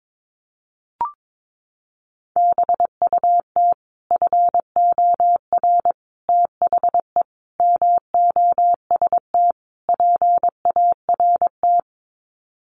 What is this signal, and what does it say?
Morse code, heard as a keyed tone, about 22 wpm, 700 Hz. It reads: BUT FOR THE MOST PART
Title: Sets of 5 Words Encore 22wpm